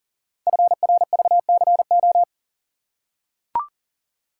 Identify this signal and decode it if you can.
Morse code, heard as a keyed tone, about 43 wpm, 700 Hz. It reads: FRVCO
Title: Code Groups - 5 Letters - 40wpm